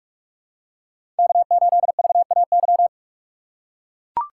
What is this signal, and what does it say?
Morse code, heard as a keyed tone, about 44 wpm, 700 Hz. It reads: K8VAY
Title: USA Call Signs - 6x Character Spacing - 45wpm